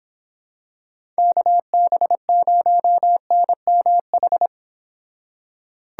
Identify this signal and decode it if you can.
Morse code, heard as a keyed tone, about 26 wpm, 700 Hz. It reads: KB0NMH